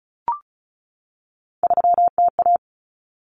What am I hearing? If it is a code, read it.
Morse code, heard as a keyed tone, about 34 wpm, 700 Hz. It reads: 3TA
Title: Common Cut Numbers - 3-Digit 35wpm